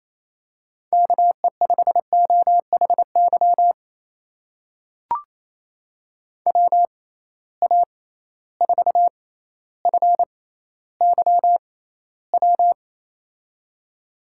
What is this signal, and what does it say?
Morse code, heard as a keyed tone, about 27 wpm, spaced as for 14 wpm, 700 Hz. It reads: KE5OHY WA4FYW
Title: USA Call Signs - 6 Characters - 6x Character Spacing - 28wpm